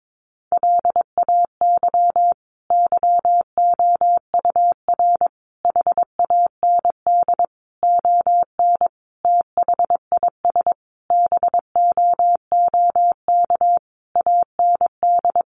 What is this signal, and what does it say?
Morse code, heard as a keyed tone, about 22 wpm, 700 Hz. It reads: LAY YOUR HAND ON THIS BOOK AND
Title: Sentences from Top 400 Words 22wpm